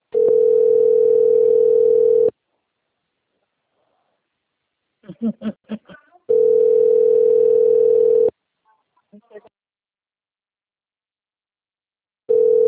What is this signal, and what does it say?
Ringback tone (North America), 440 + 480 Hz, with speech.
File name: rg-1600-unknown- - - .17225